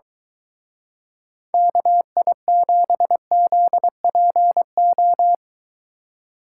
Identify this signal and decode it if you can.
Morse code, heard as a keyed tone, about 22 wpm, 700 Hz. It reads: KI7ZPO